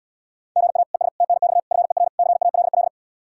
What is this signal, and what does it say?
Morse code, heard as a keyed tone, about 80 wpm, 700 Hz. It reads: GA ES TNX FER CALL